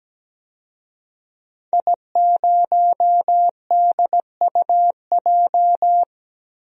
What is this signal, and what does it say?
Morse code, heard as a keyed tone, about 17 wpm, 700 Hz. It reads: I0DUJ
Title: DXCC-100 Call Signs - 5 Characters - Repeated 2x - 17wpm